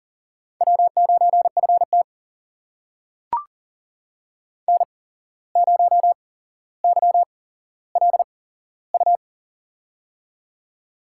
Morse code, heard as a keyed tone, about 38 wpm, spaced as for 16 wpm, 700 Hz. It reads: W9FT N0YLU
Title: USA Call Signs - 8x Character Spacing - 40wpm